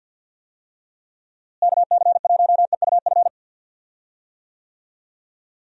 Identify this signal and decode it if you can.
Morse code, heard as a keyed tone, about 50 wpm, 700 Hz. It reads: KC1EUF
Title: USA Call Signs - 6 Characters - Repeated 2x - 50wpm